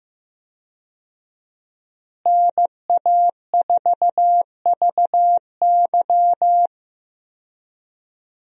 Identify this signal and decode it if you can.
Morse code, heard as a keyed tone, about 16 wpm, 700 Hz. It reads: NA4VY
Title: USA Call Signs - 5 Characters - Repeated 2x - 15wpm